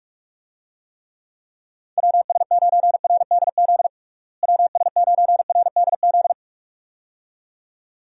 Morse code, heard as a keyed tone, about 41 wpm, 700 Hz. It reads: WS9RDZ WS9RDZ
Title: USA Call Signs - 6 Characters - Repeated 2x - 45wpm